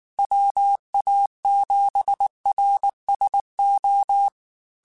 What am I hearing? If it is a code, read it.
Morse code, heard as a keyed tone, about 19 wpm, 800 Hz. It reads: WA7RSO
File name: QSOs / QSOs-mp3s / WD0BGP 1@18